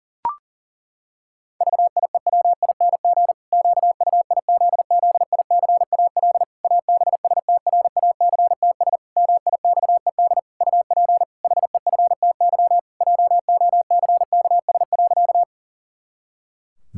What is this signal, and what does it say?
Morse code, heard as a keyed tone, about 40 wpm, 700 Hz. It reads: VIEWING QUIZZICAL ABSTRACTS MIXED UP HEFTY JOCKS.
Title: Pangram Challenge v2 - 40wpm